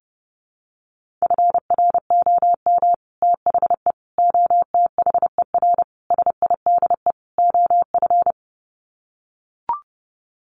Morse code, heard as a keyed tone, about 30 wpm, 700 Hz. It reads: FROM THE OTHER SIDE OF
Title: Sets of 5 Words Encore 30wpm